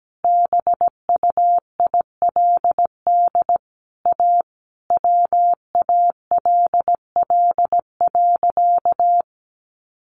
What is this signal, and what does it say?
Morse code, heard as a keyed tone, about 17 wpm, 700 Hz. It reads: BUILD A WALL.
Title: Sentences from Top 1100 Words 17wpm